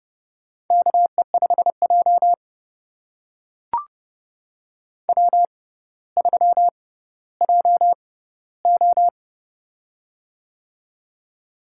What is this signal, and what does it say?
Morse code, heard as a keyed tone, about 31 wpm, spaced as for 16 wpm, 700 Hz. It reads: KE5J W3JO
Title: USA Call Signs - 4 Characters - 6x Character Spacing - 30wpm